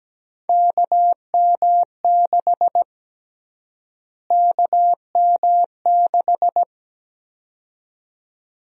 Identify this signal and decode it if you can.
Morse code, heard as a keyed tone, about 17 wpm, 700 Hz. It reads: KM6 KM6
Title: USA Call Sign Prefixes - Repeated 2x - 17wpm